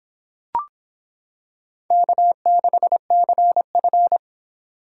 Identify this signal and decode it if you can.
Morse code, heard as a keyed tone, about 26 wpm, 700 Hz. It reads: K6CF